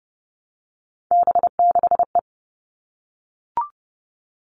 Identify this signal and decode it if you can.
Morse code, heard as a keyed tone, about 30 wpm, 700 Hz. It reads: B6E
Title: All Cut Numbers - 3-Digit 30wpm